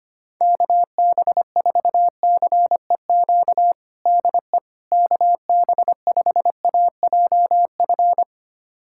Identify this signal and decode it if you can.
Morse code, heard as a keyed tone, about 25 wpm, 700 Hz. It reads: KB4CEQ DE KB5AJF
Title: USA Callsign DE Callsign 25wpm